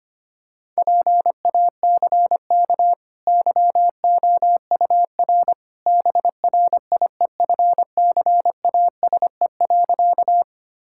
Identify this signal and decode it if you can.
Morse code, heard as a keyed tone, about 25 wpm, 700 Hz. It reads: PACK YOUR BRIEFCASE.
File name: Easy Sentences with Hints - 25wpm